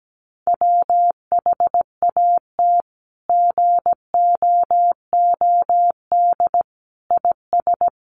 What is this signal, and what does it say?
Morse code, heard as a keyed tone, about 17 wpm, 700 Hz. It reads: WHAT GOOD IS
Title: Sentences from Top 1300 Words 17wpm